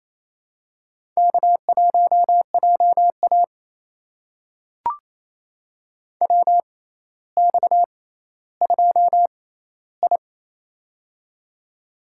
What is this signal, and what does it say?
Morse code, heard as a keyed tone, about 28 wpm, spaced as for 14 wpm, 700 Hz. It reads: K1JA WX2I
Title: USA Call Signs - 4 Characters - 6x Character Spacing - 28wpm